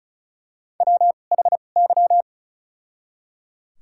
Morse code, heard as a keyed tone, about 34 wpm, spaced as for 25 wpm, 700 Hz. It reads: WHY
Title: 2x Character Spacing - Top 200 Words - 35wpm